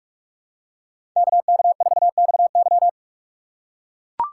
Morse code, heard as a keyed tone, about 44 wpm, 690 Hz. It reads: KK4XY
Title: USA Call Signs - 5 Characters - 4x Character Spacing - 45wpm